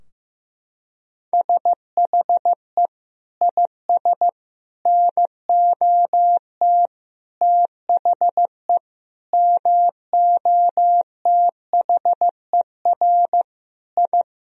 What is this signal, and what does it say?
Morse code, heard as a keyed tone, about 15 wpm, 700 Hz. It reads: SHE IS NOT THE MOTHER I
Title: Sentences from Top 1000 Words - Review Part 2 - 15wpm